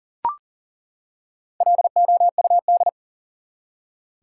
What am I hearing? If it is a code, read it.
Morse code, heard as a keyed tone, about 40 wpm, 700 Hz. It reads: LOUD
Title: Four Letter Words - 40wpm